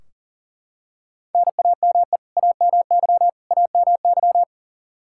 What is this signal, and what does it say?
Morse code, heard as a keyed tone, about 40 wpm, 700 Hz. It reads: NAME AMY AMY
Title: Practice QSO - Exchange 1 - Beginning - 40wpm